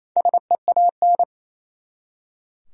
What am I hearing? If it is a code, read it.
Morse code, heard as a keyed tone, about 28 wpm, 700 Hz. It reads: SEAN
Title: USA Common Names - 28wpm